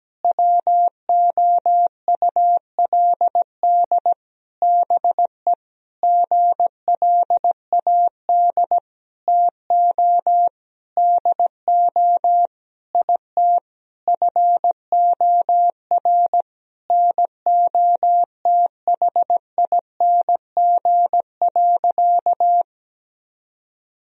Morse code, heard as a keyed tone, about 17 wpm, 700 Hz. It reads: WOULD BE GLAD TO DO IT FOR NOTHING.
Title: Sentences from Top 1000 Words - Review Part 3 - 17wpm